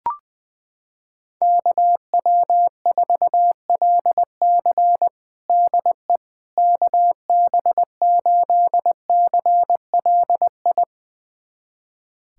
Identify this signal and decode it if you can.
Morse code, heard as a keyed tone, about 20 wpm, 700 Hz. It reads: KW4LC DE KB8CLI
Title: USA Callsign DE Callsign 20wpm